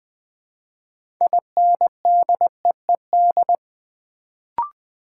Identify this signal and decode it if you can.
Morse code, heard as a keyed tone, about 21 wpm, 700 Hz. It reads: INDEED
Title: Build A Word - 2 Letters At A Time - 6-Letter Words - 20wpm